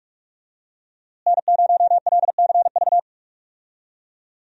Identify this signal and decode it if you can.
Morse code, heard as a keyed tone, about 46 wpm, 700 Hz. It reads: N0LCV
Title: USA Call Signs - 5 Characters - Repeated 3x - 45wpm